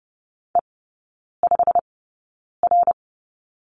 Morse code, heard as a keyed tone, about 30 wpm, spaced as for 9 wpm, 700 Hz. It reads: E5R
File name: course / 30:10wpm / 087 Lesson 15e - Number 5 - Three Characters 30wpm